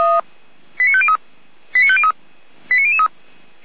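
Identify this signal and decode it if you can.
Five-tone selective call (ZVEI), heard as quick runs of five tones, tones of about 70 ms. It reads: 89503 89503 89003
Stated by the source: ZVEI five-tone call 89165 received by SDR.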